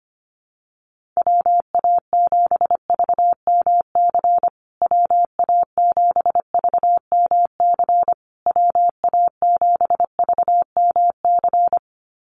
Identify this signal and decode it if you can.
Morse code, heard as a keyed tone, about 25 wpm, 700 Hz. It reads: WA7VMC WA7VMC WA7VMC
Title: USA Call Signs Repeated 3x 25wpm